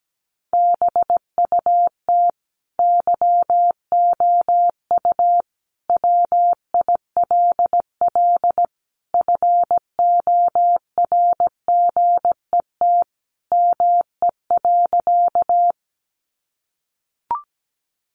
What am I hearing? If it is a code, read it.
Morse code, heard as a keyed tone, about 17 wpm, 700 Hz. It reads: BUT YOU WILL FORGET ME.
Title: Sentences from Top 1200 Words 17wpm